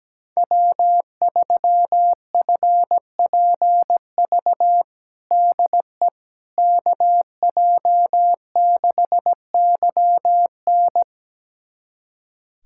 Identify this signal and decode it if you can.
Morse code, heard as a keyed tone, about 17 wpm, 700 Hz. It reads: W3FPV DE KJ6YN
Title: USA Callsign DE Callsign - 5 Characters - 17wpm